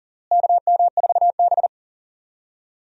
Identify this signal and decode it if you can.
Morse code, heard as a keyed tone, about 40 wpm, 700 Hz. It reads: KM4B
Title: USA Call Signs - 4 Characters - Repeated 3x - 40wpm